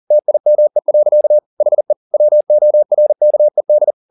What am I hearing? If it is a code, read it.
Morse code, heard as a keyed tone, about 40 wpm, 600 Hz. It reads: TIME. HE WORKED
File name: the old man and the sea 2 40wpm